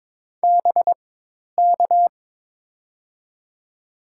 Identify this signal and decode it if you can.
Morse code, heard as a keyed tone, about 23 wpm, spaced as for 10 wpm, 700 Hz. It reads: BK
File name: Top 90 QSO Elements - 4x Character Spacing - 22wpm